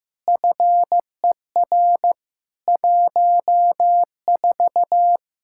Morse code, heard as a keyed tone, about 15 wpm, 700 Hz. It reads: FER 14
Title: Practice QSO - Exchange 3 - More Info - 15wpm